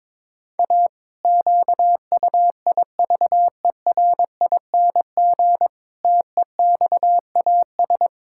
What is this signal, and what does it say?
Morse code, heard as a keyed tone, about 22 wpm, 700 Hz. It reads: A QUIVERING TEXAS
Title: Pangram Challenge v2 - 22wpm